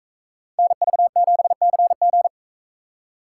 Morse code, heard as a keyed tone, about 44 wpm, 700 Hz. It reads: NV7CG